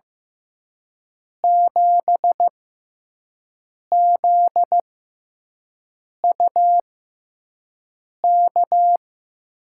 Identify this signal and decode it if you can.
Morse code, heard as a keyed tone, about 15 wpm, spaced as for 5 wpm, 700 Hz. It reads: 7ZUK
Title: USA Call Signs - 6 Characters - 6x Character Spacing - 15wpm